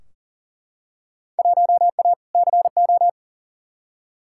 Morse code, heard as a keyed tone, about 40 wpm, 700 Hz. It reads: 1A CO
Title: Field Day Exchanges - 40wpm